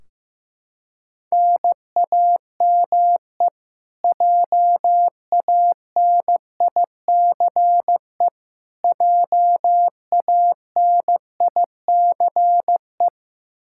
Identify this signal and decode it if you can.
Morse code, heard as a keyed tone, about 16 wpm, 700 Hz. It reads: NAME JANICE JANICE
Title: Practice QSO - Exchange 1 - Beginning - 15wpm